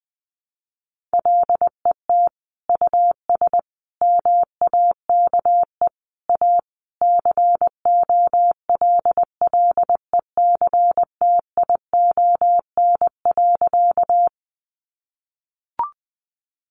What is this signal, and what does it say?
Morse code, heard as a keyed tone, about 20 wpm, 700 Hz. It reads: LET US MAKE A COLLECTION.
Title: Sentences from Top 1200 Words 20wpm